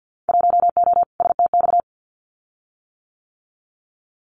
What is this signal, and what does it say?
Morse code, heard as a keyed tone, about 50 wpm, 700 Hz. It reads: 1O STX
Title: Winter Field Day Exchanges 50wpm